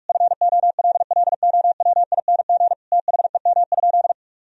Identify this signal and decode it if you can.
Morse code, heard as a keyed tone, about 42 wpm, 700 Hz. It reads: FOLLOWING THEM?
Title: Sentences from Top 1000 Words - Review Part 4 - 45wpm